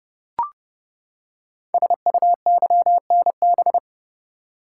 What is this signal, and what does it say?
Morse code, heard as a keyed tone, about 30 wpm, 700 Hz. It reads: SUYNB
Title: Code Groups - 5 Letters - 30wpm